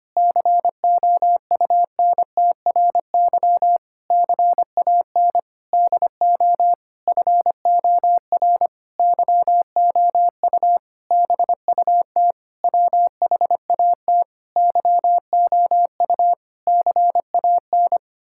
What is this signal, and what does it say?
Morse code, heard as a keyed tone, about 25 wpm, 700 Hz. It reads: COUNTRY CAN DO FOR YOU BUT WHAT YOU CAN
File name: practice / 25wpm / Sentences with 9-Word Phrases - 25wpm